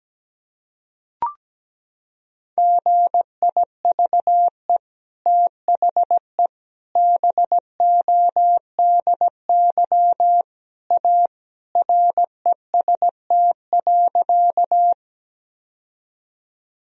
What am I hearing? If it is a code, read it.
Morse code, heard as a keyed tone, about 17 wpm, 700 Hz. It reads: GIVE THE BODY A REST.
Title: Sentences from Top 1000 Words - Review Part 2 - 17wpm